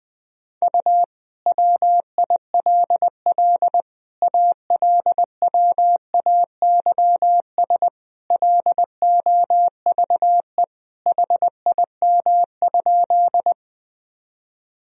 Morse code, heard as a keyed tone, about 20 wpm, 700 Hz. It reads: U WILL ALWAYS LOVE HIM?
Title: Sentences from Top 400 Words and Abbreviations 20wpm